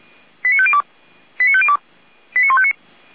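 Five-tone selective call (ZVEI), heard as quick runs of five tones, tones of about 70 ms. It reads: 89502 89502 89159
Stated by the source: Five-tone ZVEI selective call received by SDR, from feuerwehr-orlamuende.org fire brigade audio archive.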